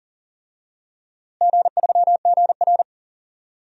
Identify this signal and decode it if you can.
Morse code, heard as a keyed tone, about 40 wpm, 700 Hz. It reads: G3GR
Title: DXCC-100 Call Signs - 4 Characters - Repeated 2x - 40wpm